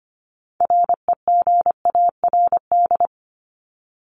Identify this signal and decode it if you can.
Morse code, heard as a keyed tone, about 25 wpm, 700 Hz. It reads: REGARD
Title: ING Suffix - Top 100 Words with Base Words 25wpm